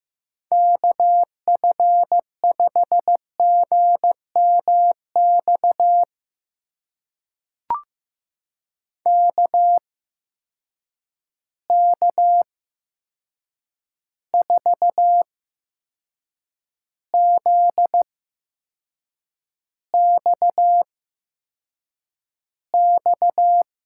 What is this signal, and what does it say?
Morse code, heard as a keyed tone, about 15 wpm, spaced as for 6 wpm, 700 Hz. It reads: KF5GMX KK4ZXX
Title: USA Call Signs - 6 Characters - 8x Character Spacing - 15wpm